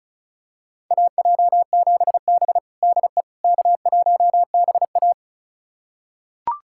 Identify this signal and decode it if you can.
Morse code, heard as a keyed tone, about 36 wpm, 700 Hz. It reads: AJ7B DE K1BA
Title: USA Callsign DE Callsign - 4 Characters - 35wpm